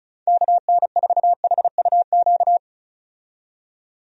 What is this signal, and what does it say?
Morse code, heard as a keyed tone, about 36 wpm, 700 Hz. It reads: KN4HUQ
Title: USA Call Signs - 6 Characters - 35wpm